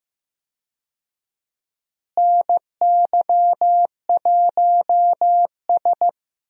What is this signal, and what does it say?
Morse code, heard as a keyed tone, about 15 wpm, 700 Hz. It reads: NY1S
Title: USA Call Signs - 4 Characters - Repeated 3x - 15wpm